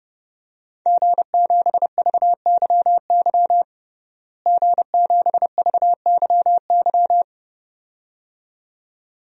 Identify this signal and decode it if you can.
Morse code, heard as a keyed tone, about 30 wpm, 700 Hz. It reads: G7VYY G7VYY
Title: DXCC-100 Call Signs - 5 Characters - Repeated 2x - 30wpm